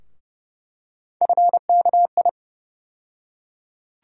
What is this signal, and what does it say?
Morse code, heard as a keyed tone, about 30 wpm, 700 Hz. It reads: FKI
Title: USA Call Sign Postfixes - 3 Letters 30wpm